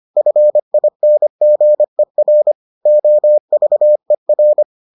Morse code, heard as a keyed tone, about 25 wpm, 600 Hz. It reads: FINGER OVER